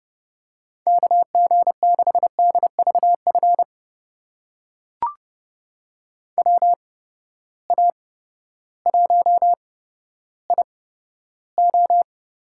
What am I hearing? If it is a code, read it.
Morse code, heard as a keyed tone, about 30 wpm, spaced as for 14 wpm, 700 Hz. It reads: KG6DVF WA1IO
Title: USA Call Signs - 6 Characters - 8x Character Spacing - 30wpm